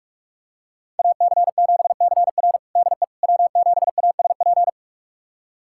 Morse code, heard as a keyed tone, about 45 wpm, 700 Hz. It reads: AC7CR DE W7ASP
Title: USA Callsign DE Callsign - 5 Characters - 45wpm